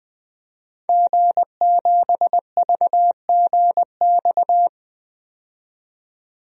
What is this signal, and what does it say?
Morse code, heard as a keyed tone, about 21 wpm, 700 Hz. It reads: G7VGX
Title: DXCC-100 Call Signs - 5 Characters - 20wpm